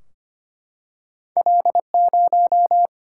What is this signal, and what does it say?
Morse code, heard as a keyed tone, about 25 wpm, 700 Hz. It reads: L0
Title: Single-Letter Single-Number 25wpm